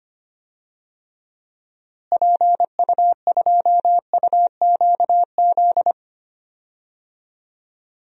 Morse code, heard as a keyed tone, about 25 wpm, 700 Hz. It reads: PU2UQZ
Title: DXCC-100 Call Signs - 6 Characters - Repeated 3x - 25wpm